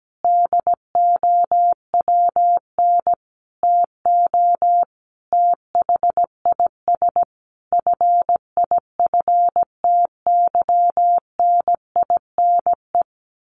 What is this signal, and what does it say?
Morse code, heard as a keyed tone, about 17 wpm, 700 Hz. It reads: DOWN TO THIS FIFTYNINE
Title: A Sound Of Thunder 17wpm